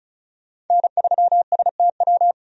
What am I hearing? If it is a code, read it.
Morse code, heard as a keyed tone, about 36 wpm, 700 Hz. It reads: N3STW
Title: USA Call Signs - 5 Characters - 35wpm